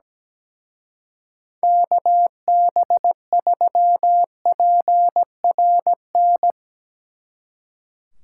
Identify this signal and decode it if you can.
Morse code, heard as a keyed tone, about 18 wpm, 700 Hz. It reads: KB3PRN
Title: Speed Racing - USA 2x3 Call Signs 17wpm